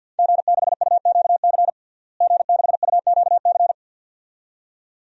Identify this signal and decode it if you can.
Morse code, heard as a keyed tone, about 51 wpm, 700 Hz. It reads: G6UQC G6UQC
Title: DXCC-100 Call Signs - 5 Characters - Repeated 2x - 50wpm